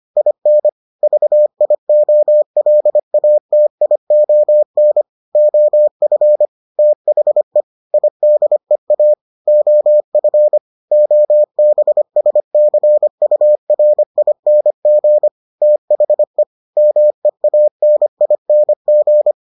Morse code, heard as a keyed tone, about 25 wpm, 600 Hz. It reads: IN VIOLATION OF THE IDEA OF OBSCURING THE MEANING